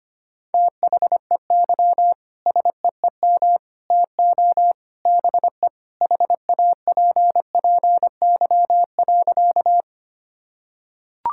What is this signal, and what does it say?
Morse code, heard as a keyed tone, about 25 wpm, 700 Hz. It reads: THEY SEEM TO BE HAPPY.
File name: practice / 25wpm / Sentences with 3-Word Phrases - 25wpm